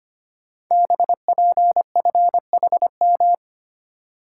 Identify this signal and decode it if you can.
Morse code, heard as a keyed tone, about 25 wpm, 700 Hz. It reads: BPFHM
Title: Code Groups - 5 Letters - 25wpm